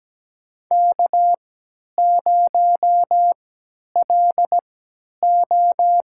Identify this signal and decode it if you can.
Morse code, heard as a keyed tone, about 17 wpm, spaced as for 10 wpm, 700 Hz. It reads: K0LO
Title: USA Call Signs - 4 Characters - 3x Character Spacing - 17wpm